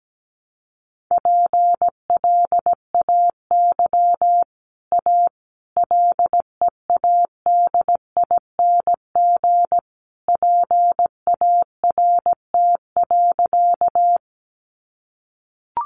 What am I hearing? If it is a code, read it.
Morse code, heard as a keyed tone, about 17 wpm, 700 Hz. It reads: PLAY A LEADING PART.
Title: Sentences from Top 1100 Words 17wpm